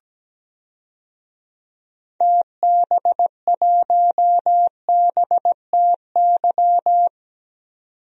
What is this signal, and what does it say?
Morse code, heard as a keyed tone, about 17 wpm, 700 Hz. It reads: TB1BTY
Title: DXCC-100 Call Signs - 6 Characters - 17wpm